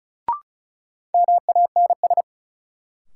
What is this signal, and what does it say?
Morse code, heard as a keyed tone, about 34 wpm, 700 Hz. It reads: MANS
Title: Sentences from Top 1000 Words - Review Part 6 - New Word Drill - 35wpm